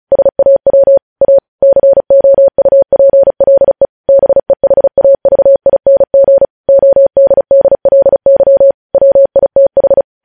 Morse code, heard as a keyed tone, about 36 wpm, 550 Hz. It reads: SAW A COUPLE BEHAVING ODDLY WITH
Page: News headlines in Morse code at 35 wpm